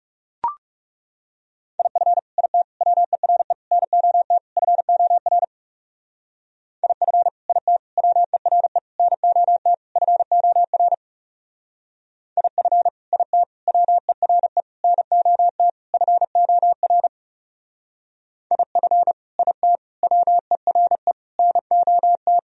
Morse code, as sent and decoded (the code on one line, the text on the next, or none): .. ..-. / .. - / .-- . .-. / -. --- - / ..-. --- .-. / .. ..-. / .. - / .-- . .-. / -. --- - / ..-. --- .-. / .. ..-. / .. - / .-- . .-. / -. --- - / ..-. --- .-. / .. ..-. / .. - / .-- . .-. / -. --- -
IF IT WERE NOT FOR IF IT WERE NOT FOR IF IT WERE NOT FOR IF IT WERE NOT